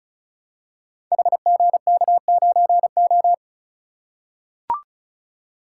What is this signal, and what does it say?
Morse code, heard as a keyed tone, about 36 wpm, 700 Hz. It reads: HGK9O
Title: Code Groups - 5 Letter:Numbers - 35wpm